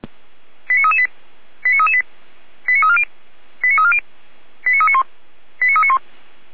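Five-tone selective call (ZVEI), heard as quick runs of five tones, tones of about 70 ms. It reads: 89338 89338 89340 89340 89391 89391
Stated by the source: ZVEI five-tone call received by SDR, from the feuerwehr-orlamuende.org fire brigade audio server.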